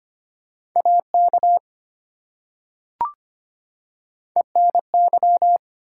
Morse code, heard as a keyed tone, about 25 wpm, 700 Hz. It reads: AK ENY
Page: morse-code-ninja-practice-25wpm